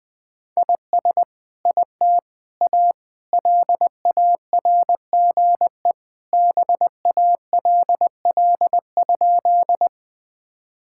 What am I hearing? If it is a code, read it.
Morse code, heard as a keyed tone, about 20 wpm, 700 Hz. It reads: IS IT A LARGE BALL?
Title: Sentences from Top 1000 Words - Review Part 6 - 20wpm